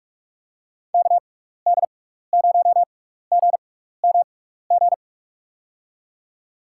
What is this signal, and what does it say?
Morse code, heard as a keyed tone, about 46 wpm, spaced as for 16 wpm, 700 Hz. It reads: KD0GMG
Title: USA Call Signs - 6x Character Spacing - 45wpm